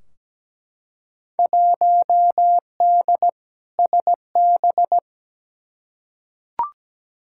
Morse code, heard as a keyed tone, about 17 wpm, 700 Hz. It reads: 1D SB
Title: Field Day Exchanges - 17wpm